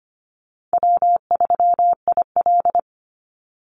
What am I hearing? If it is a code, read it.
Morse code, heard as a keyed tone, about 25 wpm, 700 Hz. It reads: W3IL
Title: Easy Call Signs Repeated 3x - 25wpm